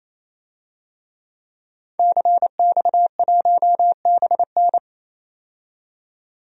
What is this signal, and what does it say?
Morse code, heard as a keyed tone, about 28 wpm, 700 Hz. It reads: CX1BN
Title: DXCC-100 Call Signs - 5 Characters - 28wpm